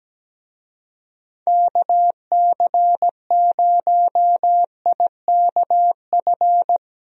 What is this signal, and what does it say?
Morse code, heard as a keyed tone, about 17 wpm, 700 Hz. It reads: KC0IKF
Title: USA Call Signs - 6 Characters - Repeated 3x - 17wpm